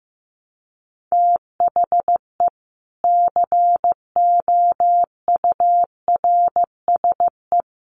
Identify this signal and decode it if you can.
Morse code, heard as a keyed tone, about 15 wpm, 700 Hz. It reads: THE COURSE
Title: Sets of 4 Words 15wpm